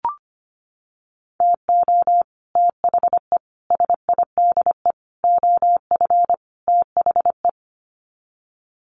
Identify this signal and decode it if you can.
Morse code, heard as a keyed tone, about 25 wpm, 700 Hz. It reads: TO THE SIDE OF THE
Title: Sets of 5 Words 25wpm